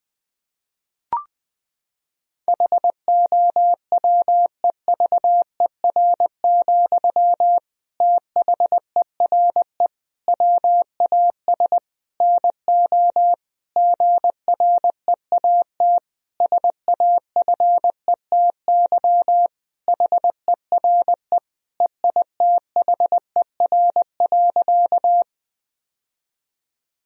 Morse code, heard as a keyed tone, about 19 wpm, 700 Hz. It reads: HOWEVER, THERE WAS NO GREAT SAFETY HERE EITHER.
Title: Sentences from Top 1000 Words - Review Part 4 - 20wpm